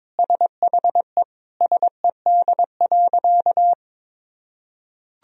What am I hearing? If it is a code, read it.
Morse code, heard as a keyed tone, about 22 wpm, 700 Hz. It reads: SHE SED.
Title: Sentences from Top 200 Words and Abbreviations 22wpm